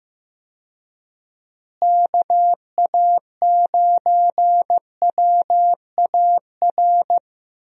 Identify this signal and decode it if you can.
Morse code, heard as a keyed tone, about 15 wpm, 700 Hz. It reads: KA9WAR
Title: USA Call Signs - 6 Characters - Repeated 2x - 15wpm